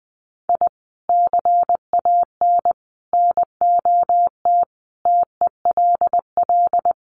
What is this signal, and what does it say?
Morse code, heard as a keyed tone, about 20 wpm, 700 Hz. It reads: I CAN NOT TELL
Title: Sets of 5 Words 20wpm